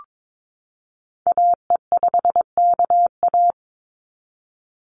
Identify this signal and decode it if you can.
Morse code, heard as a keyed tone, about 22 wpm, 700 Hz. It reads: AE5KA
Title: NA SOTA Activator Callsigns 22wpm